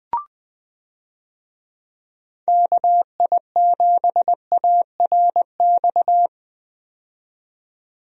Morse code, heard as a keyed tone, about 20 wpm, 700 Hz. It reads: KI7ARX
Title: USA Call Signs - 6 Characters - 20wpm